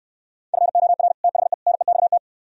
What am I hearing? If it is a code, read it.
Morse code, heard as a keyed tone, about 68 wpm, 700 Hz. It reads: FOR THE NEXT